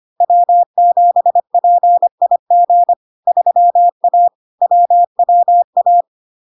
Morse code, heard as a keyed tone, about 25 wpm, 700 Hz. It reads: W7PIG 3A WWA
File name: Log-in-CW-25wpm-700-Hz